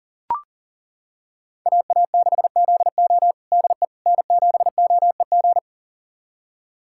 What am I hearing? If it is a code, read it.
Morse code, heard as a keyed tone, about 40 wpm, 700 Hz. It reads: AA6ZO DE N7OEG
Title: USA Callsign DE Callsign - 5 Characters - 40wpm